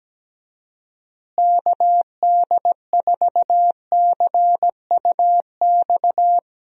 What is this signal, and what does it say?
Morse code, heard as a keyed tone, about 17 wpm, 700 Hz. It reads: KD4CUX
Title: USA Call Signs - 6 Characters - Repeated 2x - 17wpm